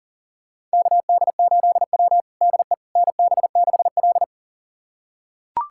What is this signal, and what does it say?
Morse code, heard as a keyed tone, about 40 wpm, 700 Hz. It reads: KD8W DE NB6L
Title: USA Callsign DE Callsign - 4 Characters - 40wpm